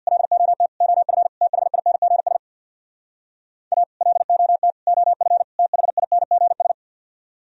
Morse code, heard as a keyed tone, about 64 wpm, 690 Hz. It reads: LOT OF THINGS A LOT OF THINGS